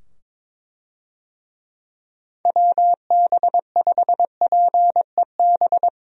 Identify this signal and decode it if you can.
Morse code, heard as a keyed tone, about 22 wpm, 700 Hz. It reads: WB5PEB
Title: USA Call Signs - 6 Characters - 22wpm